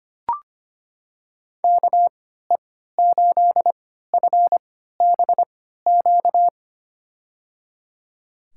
Morse code, heard as a keyed tone, about 25 wpm, spaced as for 14 wpm, 690 Hz. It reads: KE8FBQ
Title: USA Call Signs - 6 Characters - 3x Character Spacing - 25wpm